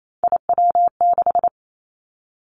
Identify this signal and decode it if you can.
Morse code, heard as a keyed tone, about 28 wpm, 700 Hz. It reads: IW6
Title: Three Challenging Letter-Number Combinations 28wpm